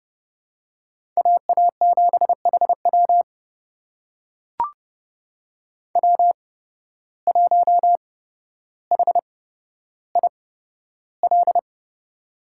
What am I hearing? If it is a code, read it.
Morse code, heard as a keyed tone, about 30 wpm, spaced as for 13 wpm, 700 Hz. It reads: AA7HW W1HIL
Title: USA Call Signs - 5 Characters - 8x Character Spacing - 30wpm